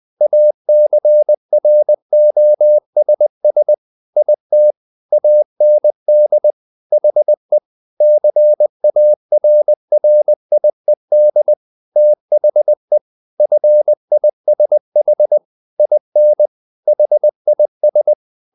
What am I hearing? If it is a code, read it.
Morse code, heard as a keyed tone, about 20 wpm, 600 Hz. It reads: ACROSS IT AND HE CARRIED THE FISH IN HIS
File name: the old man and the sea part 2 20wpm